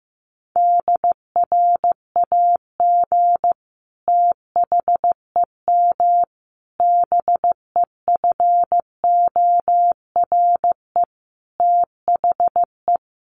Morse code, heard as a keyed tone, about 15 wpm, 700 Hz. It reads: DRAG THEM BEFORE THE
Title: Sentences from Top 1500 Words 15wpm